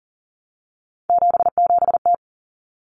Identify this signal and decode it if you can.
Morse code, heard as a keyed tone, about 40 wpm, 700 Hz. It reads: 77T
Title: Common Cut Numbers - 3-Digit 40wpm